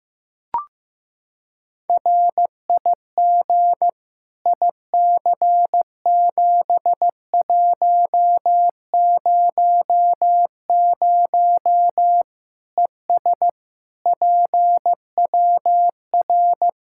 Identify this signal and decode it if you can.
Morse code, heard as a keyed tone, about 15 wpm, 700 Hz. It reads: RIG IC7100 ES PWR
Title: Practice QSO - Exchange 2 - Info - 15wpm